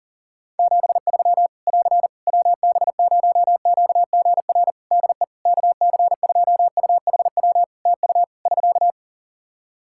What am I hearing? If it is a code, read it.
Morse code, heard as a keyed tone, about 40 wpm, 700 Hz. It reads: 73 <AR> WB0QGR DE KC2UHW TU <SK>
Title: Practice QSO - Exchange 4 - Ending - 40wpm